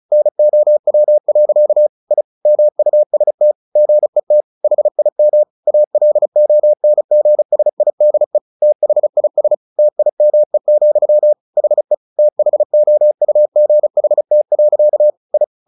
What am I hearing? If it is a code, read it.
Morse code, heard as a keyed tone, about 34 wpm, 600 Hz. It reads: NOW. I MUST GET HIM ALONGSIDE THIS TIME, HE THOUGHT. I
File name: the old man and the sea 2 35wpm